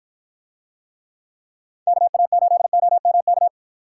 Morse code, heard as a keyed tone, about 51 wpm, 700 Hz. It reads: KA8OMK